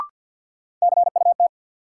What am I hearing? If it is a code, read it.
Morse code, heard as a keyed tone, about 51 wpm, 700 Hz. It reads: CUT
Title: Sentences from Top 300 Words - New Word Drill - 50wpm